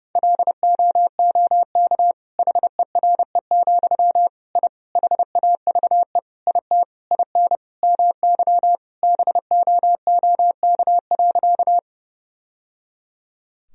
Morse code, heard as a keyed tone, about 32 wpm, 700 Hz. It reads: LOOK HERE, I HAVE IT IN MY BOOK.
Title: Sentences from Top 1000 Words - Review Part 1 - 30wpm